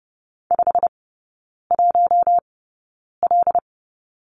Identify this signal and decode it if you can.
Morse code, heard as a keyed tone, about 30 wpm, spaced as for 9 wpm, 700 Hz. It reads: 51L
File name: course / 30:10wpm / 087 Lesson 15e - Number 5 - Three Characters 30wpm